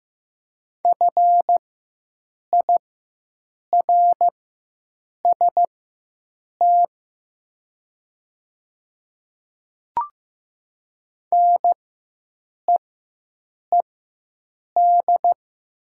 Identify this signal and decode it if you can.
Morse code, heard as a keyed tone, about 15 wpm, spaced as for 7 wpm, 700 Hz. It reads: FIRST NEED
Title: 4x Character Spacing - Top 300 Words - 15wpm